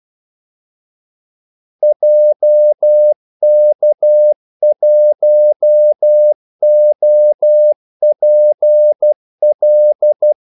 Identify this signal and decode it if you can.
Morse code, heard as a keyed tone, about 12 wpm, 600 Hz. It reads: JK1OPL
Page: 2 ekstra ordmellemrum mellem kaldesignalerne